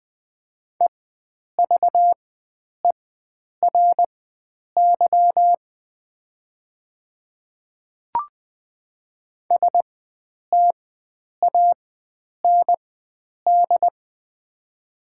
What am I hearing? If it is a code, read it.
Morse code, heard as a keyed tone, about 20 wpm, spaced as for 9 wpm, 700 Hz. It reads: EVERY STAND
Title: 4x Character Spacing - Top 300 Words - 20wpm